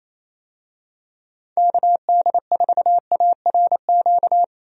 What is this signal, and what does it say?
Morse code, heard as a keyed tone, about 27 wpm, 700 Hz. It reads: KD4ARQ
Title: USA Call Signs - 6 Characters - 28wpm